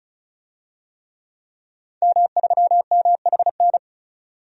Morse code, heard as a keyed tone, about 34 wpm, 700 Hz. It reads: M3MHN
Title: DXCC-100 Call Signs - 5 Characters - 35wpm